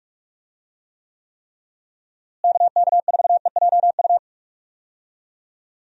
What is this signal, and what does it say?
Morse code, heard as a keyed tone, about 42 wpm, 700 Hz. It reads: KK4EJU
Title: USA Call Signs - 6 Characters - Repeated 3x - 45wpm